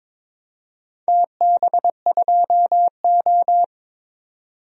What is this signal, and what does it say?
Morse code, heard as a keyed tone, about 22 wpm, 700 Hz. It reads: TB2O
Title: DXCC-100 Call Signs - 4 Characters - 22wpm